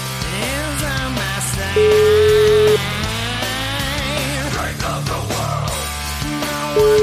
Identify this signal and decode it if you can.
Ringback tone (Europe, 425 Hz), with noise.